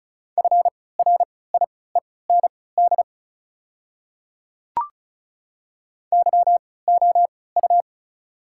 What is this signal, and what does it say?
Morse code, heard as a keyed tone, about 34 wpm, spaced as for 20 wpm, 690 Hz. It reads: FRIEND YOU
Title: 3x Character Spacing - Top 300 Words - 35wpm